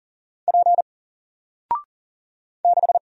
Morse code, heard as a keyed tone, about 41 wpm, 700 Hz. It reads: P 6
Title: Single Letter-Number Alt 40wpm